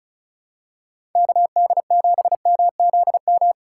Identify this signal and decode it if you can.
Morse code, heard as a keyed tone, about 36 wpm, 700 Hz. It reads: KD7MZM
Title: USA Call Signs - 6 Characters - 3x Character Spacing - 35wpm